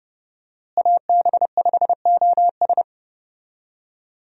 Morse code, heard as a keyed tone, about 32 wpm, 700 Hz. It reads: AB5OS
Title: USA Call Signs - 5 Characters - Repeated 2x - 30wpm